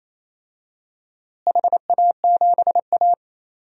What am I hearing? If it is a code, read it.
Morse code, heard as a keyed tone, about 28 wpm, 700 Hz. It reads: HA7A
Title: DXCC-100 Call Signs - 4 Characters - Repeated 3x - 28wpm